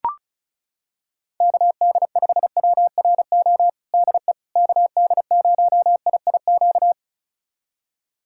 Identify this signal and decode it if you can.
Morse code, heard as a keyed tone, about 34 wpm, 700 Hz. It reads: KD5WRO DE KD0IIQ
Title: USA Callsign DE Callsign - 6 Characters - 35wpm